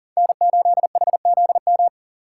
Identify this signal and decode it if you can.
Morse code, heard as a keyed tone, about 40 wpm, 700 Hz. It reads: N8HZM
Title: K1USN SST QSO Practice - 40wpm